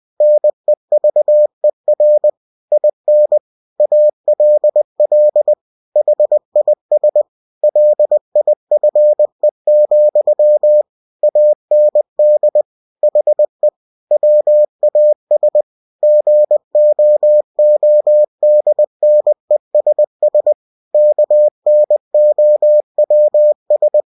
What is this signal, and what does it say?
Morse code, heard as a keyed tone, about 20 wpm, 600 Hz. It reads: NEVER IN ALL HIS LIFE, AND HE WAS GOODNESS KNOWS
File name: cw-books / winnie-the-pooh / winnie-20wpm-page-0090